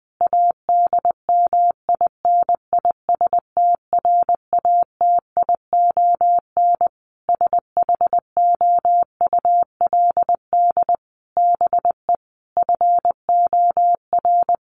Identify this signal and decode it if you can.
Morse code, heard as a keyed tone, about 20 wpm, 700 Hz. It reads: ADMINISTRATION SHOULD BE FOR
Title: Sentences from Top 1300 Words 20wpm